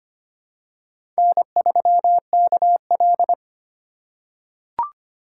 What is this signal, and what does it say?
Morse code, heard as a keyed tone, about 25 wpm, 700 Hz. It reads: N3KL
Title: USA Call Signs - 4 Characters - 2x Character Spacing - 25wpm